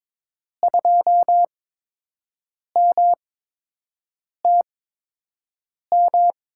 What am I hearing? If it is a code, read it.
Morse code, heard as a keyed tone, about 21 wpm, spaced as for 6 wpm, 700 Hz. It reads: 2MTM
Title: USA Call Signs - 6 Characters - 8x Character Spacing - 22wpm